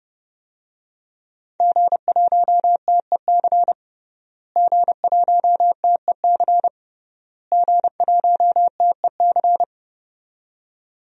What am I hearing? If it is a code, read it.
Morse code, heard as a keyed tone, about 32 wpm, 700 Hz. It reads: G1TEC G1TEC G1TEC
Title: DXCC-100 Call Signs - 5 Characters - Repeated 3x - 30wpm